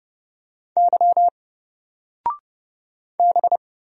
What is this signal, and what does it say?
Morse code, heard as a keyed tone, about 30 wpm, 700 Hz. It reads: Y B
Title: Single Letter-Number 30wpm